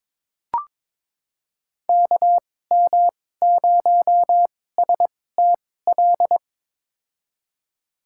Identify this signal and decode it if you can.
Morse code, heard as a keyed tone, about 21 wpm, spaced as for 16 wpm, 700 Hz. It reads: KM0STL
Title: USA Call Signs - 6 Characters - 2x Character Spacing - 22wpm